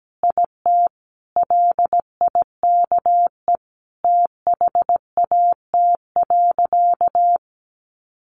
Morse code, heard as a keyed tone, about 17 wpm, 700 Hz. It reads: IT LIKE THAT.
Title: Sentences from Top 300 Words 17wpm